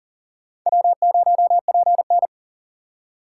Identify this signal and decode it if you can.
Morse code, heard as a keyed tone, about 54 wpm, 700 Hz. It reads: W0PN